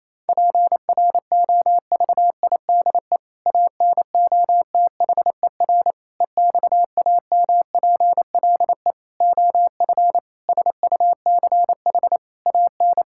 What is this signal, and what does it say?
Morse code, heard as a keyed tone, about 28 wpm, 700 Hz. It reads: PROVIDE ANOTHER EXAMPLE OF SUCH AN
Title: Sentences from Top 1000 Words - Review Part 5 - 28wpm